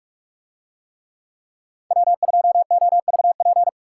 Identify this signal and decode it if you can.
Morse code, heard as a keyed tone, about 46 wpm, 700 Hz. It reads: W2OVP